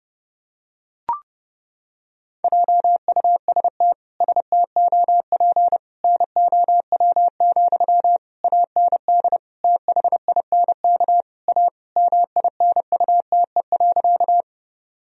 Morse code, heard as a keyed tone, about 31 wpm, 700 Hz. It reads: JUST STOP NOW, AND THINK A MINUTE.
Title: Sentences from Top 1000 Words - Review Part 1 - 30wpm